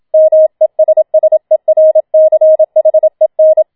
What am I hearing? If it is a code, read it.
Morse code, heard as a keyed tone, about 27 wpm, 630 Hz. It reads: MESSERCHEN